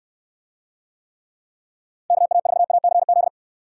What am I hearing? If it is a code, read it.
Morse code, heard as a keyed tone, about 69 wpm, 700 Hz. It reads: KA4ACZ